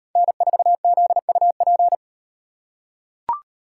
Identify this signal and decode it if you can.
Morse code, heard as a keyed tone, about 36 wpm, 700 Hz. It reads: N4ZUP